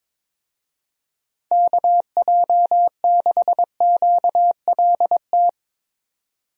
Morse code, heard as a keyed tone, about 22 wpm, 700 Hz. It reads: KJ6QLT
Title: USA Call Signs - 6 Characters - 22wpm